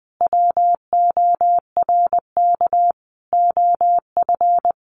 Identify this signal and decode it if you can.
Morse code, heard as a keyed tone, about 20 wpm, 700 Hz. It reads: WORK OF
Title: Sentences from Top 600 Words 20wpm